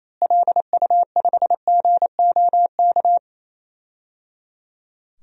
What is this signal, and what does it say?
Morse code, heard as a keyed tone, about 28 wpm, 700 Hz. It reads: LU5GOK
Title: DXCC-100 Call Signs - 6 Characters - Repeated 3x - 28wpm